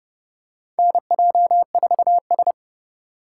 Morse code, heard as a keyed tone, about 30 wpm, 700 Hz. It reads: NJ4S
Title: USA Call Signs - 2x Character Spacing - 30wpm